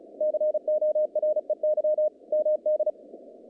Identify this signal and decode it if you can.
Morse code, heard as a keyed tone, about 36 wpm, 610 Hz. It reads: COREY MD
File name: contests / naqp-cw-k5tr-op-k5pi- -2100